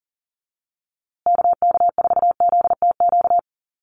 Morse code, heard as a keyed tone, about 40 wpm, 700 Hz. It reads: KK4ZTQ
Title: USA Call Signs Repeated 3x 40wpm